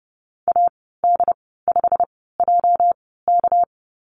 Morse code, heard as a keyed tone, about 32 wpm, spaced as for 17 wpm, 700 Hz. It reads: AD5JK
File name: Lesson 37g - Letter x - Call Signs 30wpm